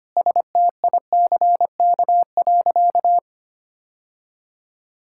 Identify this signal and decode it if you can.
Morse code, heard as a keyed tone, about 25 wpm, 700 Hz. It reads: STICK.
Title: Sentences from Top 1600 Words 25wpm